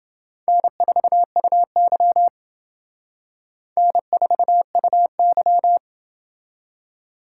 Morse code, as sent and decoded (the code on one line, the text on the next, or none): -. ....- ..- -.-- / -. ....- ..- -.--
N4UY N4UY